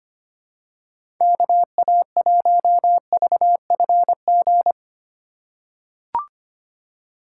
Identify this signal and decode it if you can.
Morse code, heard as a keyed tone, about 25 wpm, 690 Hz. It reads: KA1VFG
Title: USA Call Signs - 6 Characters - 3x Character Spacing - 25wpm